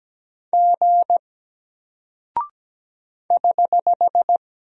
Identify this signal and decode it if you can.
Morse code, heard as a keyed tone, about 17 wpm, 700 Hz. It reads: G <HH>
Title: Single Character (Letter:Number:Punctuation:Prosign) Alt 17wpm